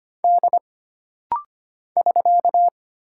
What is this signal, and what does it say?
Morse code, heard as a keyed tone, about 25 wpm, 700 Hz. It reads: D <SK>
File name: Single Character (Letter:Number:Punctuation:Prosign) - Rapid-Fire - 25wpm